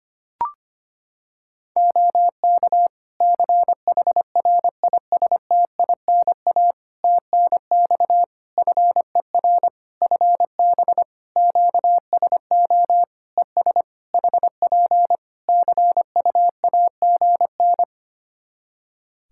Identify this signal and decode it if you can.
Morse code, heard as a keyed tone, about 25 wpm, 700 Hz. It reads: OK CHRISTINA TNX FER FB QSO ES HP CUAGN
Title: Practice QSO - Exchange 4 - Ending - 25wpm